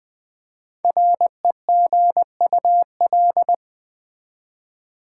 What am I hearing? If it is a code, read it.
Morse code, heard as a keyed tone, about 20 wpm, 700 Hz. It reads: REGUL
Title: Build A Word - 5 Letters At A Time - 10-Letter Words - 20wpm